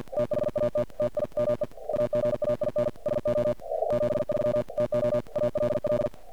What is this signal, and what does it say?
Morse code, heard as a keyed tone, about 50 wpm, 590 Hz, with noise. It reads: THAT TIME AGAIN SO E 73 TO ALL
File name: archived / CRSnet